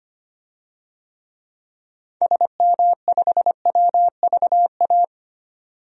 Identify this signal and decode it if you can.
Morse code, heard as a keyed tone, about 25 wpm, 700 Hz. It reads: SM5WVA
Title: DXCC-100 Call Signs - 6 Characters - 25wpm